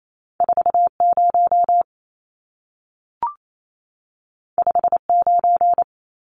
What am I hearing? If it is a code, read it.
Morse code, heard as a keyed tone, about 28 wpm, 700 Hz. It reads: 40 59
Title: Numbers - 2-Digit 28wpm